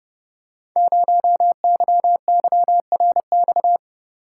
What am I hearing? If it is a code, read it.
Morse code, heard as a keyed tone, about 28 wpm, 700 Hz. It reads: 0YYRX
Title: Code Groups - 5 Letter:Numbers - 30wpm